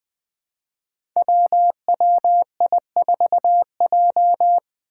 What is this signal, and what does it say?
Morse code, heard as a keyed tone, about 20 wpm, 700 Hz. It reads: WWI4J
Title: Five Challenging Letter-Number Combinations 20wpm